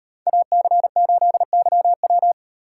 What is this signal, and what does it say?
Morse code, heard as a keyed tone, about 34 wpm, 700 Hz. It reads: AC8YW